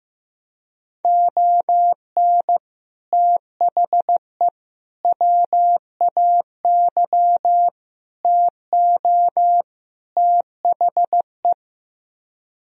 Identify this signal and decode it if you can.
Morse code, heard as a keyed tone, about 15 wpm, 700 Hz. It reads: ON THE WAY TO THE
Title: Speed-Racing - Sets of 5 Words 15wpm